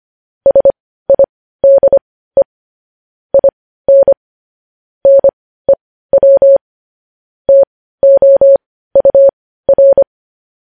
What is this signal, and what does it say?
Morse code, heard as a keyed tone, about 25 wpm, spaced as for 15 wpm, 550 Hz. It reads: SIDE IN NEW TOUR
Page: news headlines in morse code at 15 wpm